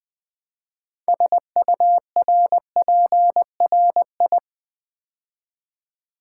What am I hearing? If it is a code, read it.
Morse code, heard as a keyed tone, about 20 wpm, 700 Hz. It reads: SURPRI
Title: Build A Word - 3 Letters At A Time - 8-Letter Words - 20wpm